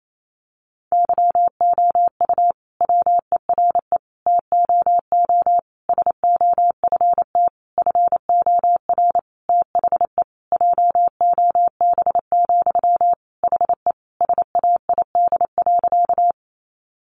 Morse code, heard as a keyed tone, about 28 wpm, 700 Hz. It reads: YOU WERE TOO SOFT FOR THE JOB, HE SAID.
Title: Sentences from Top 1500 Words 28wpm